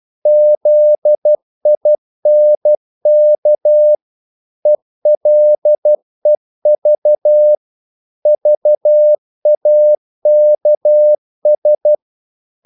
Morse code, heard as a keyed tone, about 12 wpm, 600 Hz. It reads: ZINK ELEV VAKS
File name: Korte ord DK 12wpm